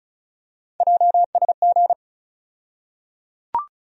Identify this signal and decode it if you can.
Morse code, heard as a keyed tone, about 36 wpm, 700 Hz. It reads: JSG
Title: USA Call Sign Postfixes - 3 Letters 35wpm